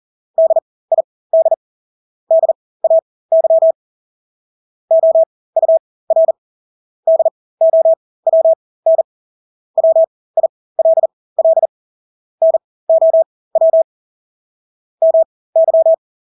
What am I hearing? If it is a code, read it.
Morse code, heard as a keyed tone, about 40 wpm, spaced as for 20 wpm, 650 Hz. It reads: DID DAY OUR DOWN WILL NOW MY
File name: Engelske ord 40-20 wpm